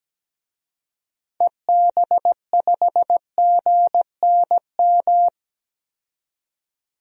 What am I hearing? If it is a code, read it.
Morse code, heard as a keyed tone, about 17 wpm, 700 Hz. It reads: EB5GNM
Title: DXCC-100 Call Signs - 6 Characters - 17wpm